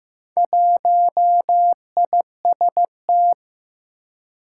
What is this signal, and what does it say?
Morse code, heard as a keyed tone, about 15 wpm, 700 Hz. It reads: 1IST
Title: Four Challenging Letter-Number Combinations 15wpm